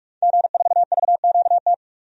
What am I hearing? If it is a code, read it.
Morse code, heard as a keyed tone, about 46 wpm, 700 Hz. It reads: G4VQT
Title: DXCC-100 Call Signs - 5 Characters - 45wpm